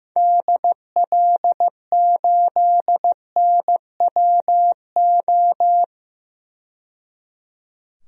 Morse code, heard as a keyed tone, about 15 wpm, 700 Hz. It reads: DL8NWO
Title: DXCC-100 Call Signs - 6 Characters - 15wpm